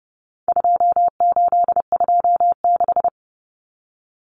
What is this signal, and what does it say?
Morse code, heard as a keyed tone, about 30 wpm, 700 Hz. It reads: 2826
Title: Numbers - 4-Digit 30wpm